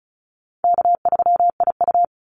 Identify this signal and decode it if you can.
Morse code, heard as a keyed tone, about 34 wpm, 700 Hz. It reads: K3IU
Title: Call Signs - 35wpm